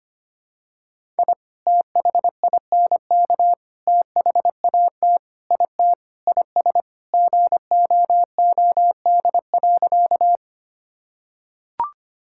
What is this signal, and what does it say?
Morse code, heard as a keyed tone, about 25 wpm, 700 Hz. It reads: I THINK THAT IT IS GOOD.
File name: practice / 25wpm / Sentences with 3-Word Phrases - 25wpm